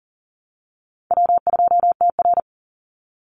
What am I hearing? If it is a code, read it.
Morse code, heard as a keyed tone, about 40 wpm, 700 Hz. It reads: W2TR
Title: Easy Call Signs Repeated 2x - 40wpm